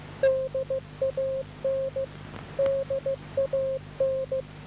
Morse code, heard as a keyed tone, about 15 wpm, 530 Hz. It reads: DAN DAN
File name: DAN 12898.5 1139